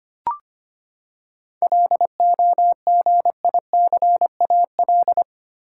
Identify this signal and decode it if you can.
Morse code, heard as a keyed tone, about 25 wpm, 700 Hz. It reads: LOGICAL
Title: Seven Letter Words - 25wpm